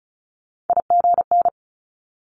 Morse code, heard as a keyed tone, about 36 wpm, 700 Hz. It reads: IGN
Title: Multi-letter Phonemes 35wpm